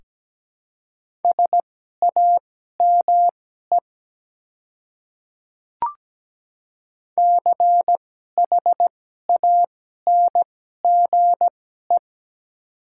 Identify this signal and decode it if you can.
Morse code, heard as a keyed tone, about 17 wpm, spaced as for 12 wpm, 700 Hz. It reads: SAME CHANGE
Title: 2x Character Spacing - Top 200 Words - 17wpm